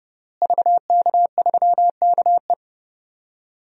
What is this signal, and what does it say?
Morse code, heard as a keyed tone, about 30 wpm, 700 Hz. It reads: VK3KE
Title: DXCC-100 Call Signs - 5 Characters - Repeated 3x - 30wpm